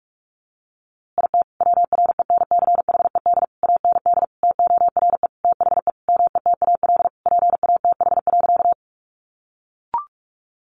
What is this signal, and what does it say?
Morse code, heard as a keyed tone, about 44 wpm, 700 Hz. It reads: IT WRENCHED AND TORE THE METAL PATH.
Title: A Sound Of Thunder 45wpm